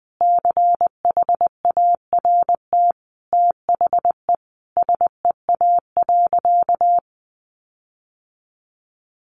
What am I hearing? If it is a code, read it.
Morse code, heard as a keyed tone, about 20 wpm, 700 Hz. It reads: CHART THE SEA.
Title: Sentences from Top 1300 Words 20wpm